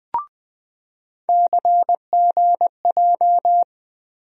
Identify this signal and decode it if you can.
Morse code, heard as a keyed tone, about 20 wpm, 700 Hz. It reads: CGJ
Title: USA Call Sign Postfixes - 3 Letters 20wpm